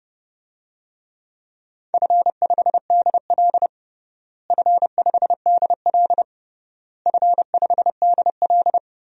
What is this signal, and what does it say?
Morse code, heard as a keyed tone, about 30 wpm, 700 Hz. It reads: F5DL F5DL F5DL
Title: DXCC-100 Call Signs - 4 Characters - Repeated 3x - 30wpm